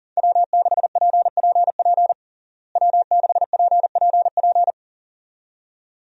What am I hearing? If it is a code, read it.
Morse code, heard as a keyed tone, about 40 wpm, 700 Hz. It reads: W6PPP W6PPP
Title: USA Call Signs - 5 Characters - Repeated 2x - 40wpm